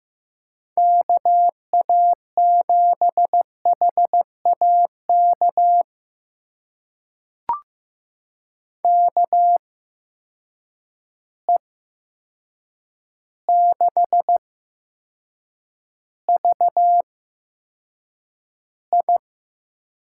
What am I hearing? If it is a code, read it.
Morse code, heard as a keyed tone, about 15 wpm, spaced as for 7 wpm, 700 Hz. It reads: KA7HAK KE6VI
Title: USA Call Signs - 6 Characters - 8x Character Spacing - 15wpm